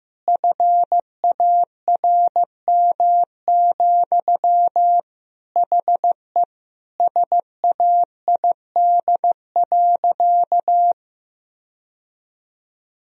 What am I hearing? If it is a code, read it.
Morse code, heard as a keyed tone, about 15 wpm, 700 Hz. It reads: FARM, HE SAID.
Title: Sentences from Top 1000 Words - Review Part 2 - 15wpm